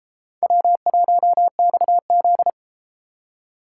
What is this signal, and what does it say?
Morse code, heard as a keyed tone, about 31 wpm, 700 Hz. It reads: W1XZ